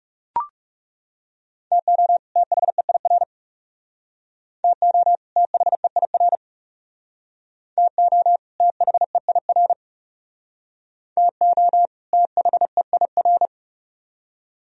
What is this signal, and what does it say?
Morse code, heard as a keyed tone, about 39 wpm, 700 Hz. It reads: TO THEIR TO THEIR TO THEIR TO THEIR